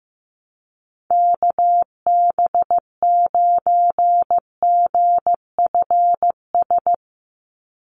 Morse code, heard as a keyed tone, about 15 wpm, 700 Hz. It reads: KB9GFS
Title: USA Call Signs 15wpm